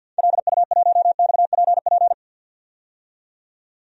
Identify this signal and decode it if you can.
Morse code, heard as a keyed tone, about 49 wpm, 700 Hz. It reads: LU1XPP
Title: DXCC-100 Call Signs - 6 Characters - 50wpm